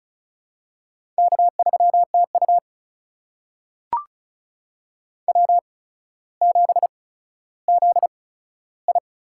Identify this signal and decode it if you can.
Morse code, heard as a keyed tone, about 35 wpm, spaced as for 15 wpm, 700 Hz. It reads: K3TU W7ZI
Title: USA Call Signs - 4 Characters - 8x Character Spacing - 35wpm